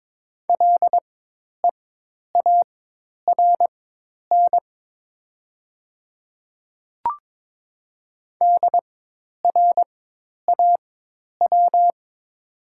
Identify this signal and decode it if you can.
Morse code, heard as a keyed tone, about 22 wpm, spaced as for 10 wpm, 700 Hz. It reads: LEARN DRAW
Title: 4x Character Spacing - Top 500 Words - 22wpm